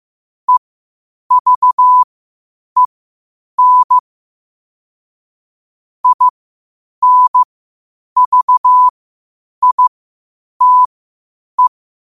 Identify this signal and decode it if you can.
Morse code, heard as a keyed tone, about 15 wpm, spaced as for 8 wpm, 1000 Hz. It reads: EVEN INVITE
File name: qotdfiles / 8wpm